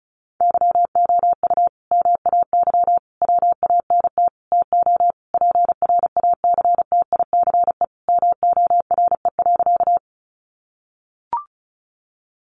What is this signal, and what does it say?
Morse code, heard as a keyed tone, about 34 wpm, 700 Hz. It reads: YOU MAY WANT TO PRACTICE MORE.
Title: Sentences from Top 700 Words 35wpm